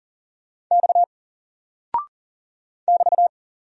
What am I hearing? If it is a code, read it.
Morse code, heard as a keyed tone, about 40 wpm, 700 Hz. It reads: X <BT>
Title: Single Character (Letter:Number:Punctuation:Prosign) 40wpm